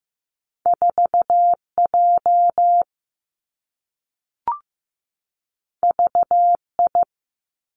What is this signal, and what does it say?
Morse code, heard as a keyed tone, about 15 wpm, 700 Hz. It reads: 4 J V I
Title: Two Challenging Letter-Number Combinations 15wpm